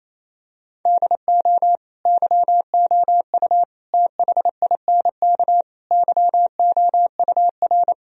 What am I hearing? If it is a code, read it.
Morse code, heard as a keyed tone, about 28 wpm, 700 Hz. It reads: DO YOU THINK YOUR
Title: Sentences from Top 1000 Words - Review Part 1 - 28wpm